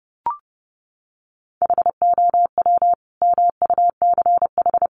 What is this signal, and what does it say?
Morse code, heard as a keyed tone, about 30 wpm, 700 Hz. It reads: HOW MUCH
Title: Sentences from Top 200 Words 30wpm